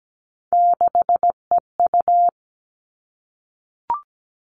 Morse code, heard as a keyed tone, about 17 wpm, 700 Hz. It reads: 6EU
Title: All Cut Numbers - 3-Digit 17wpm